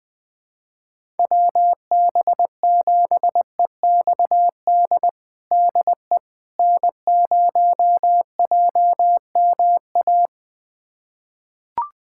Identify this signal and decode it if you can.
Morse code, heard as a keyed tone, about 20 wpm, 700 Hz. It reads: WB7EXD DE N0JMA
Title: USA Callsign DE Callsign 20wpm